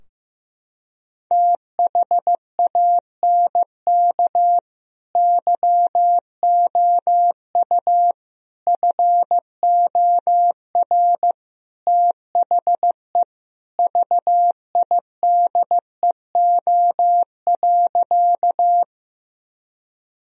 Morse code, heard as a keyed tone, about 15 wpm, 700 Hz. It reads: THANK YOU FOR THE VIDEO.
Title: Sentences from Top 1000 Words - Review Part 4 - 15wpm